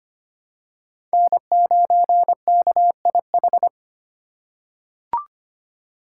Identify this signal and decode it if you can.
Morse code, heard as a keyed tone, about 25 wpm, 700 Hz. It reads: N9KIH
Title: USA Call Signs - 4x Character Spacing - 25wpm